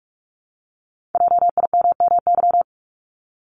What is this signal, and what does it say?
Morse code, heard as a keyed tone, about 46 wpm, 700 Hz. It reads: JIMMY
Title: Multi-letter Phonemes Repeated 3x 45wpm